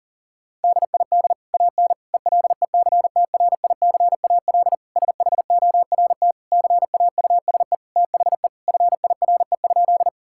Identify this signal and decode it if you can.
Morse code, heard as a keyed tone, about 37 wpm, 700 Hz. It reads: DID AN ELECTRICAL SHORT CAUSE THE FIRE?
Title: Sentences from Top 1000 Words - Review Part 6 - 40wpm